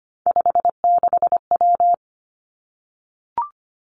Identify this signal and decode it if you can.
Morse code, heard as a keyed tone, about 25 wpm, 700 Hz. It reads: 56W
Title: Three Challenging Letter-Number Combinations 25wpm